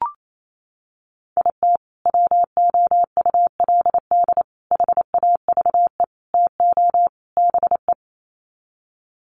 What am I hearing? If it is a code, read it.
Morse code, heard as a keyed tone, about 28 wpm, 700 Hz. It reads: IT WOULD HAVE TO BE
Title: Sets of 5 Words Encore 28wpm